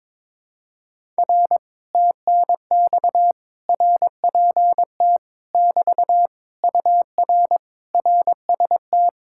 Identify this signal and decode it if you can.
Morse code, heard as a keyed tone, about 22 wpm, 700 Hz. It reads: R TNX RPT <BT> UR RST
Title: Speed-Racing Common QSO Phrases - 22wpm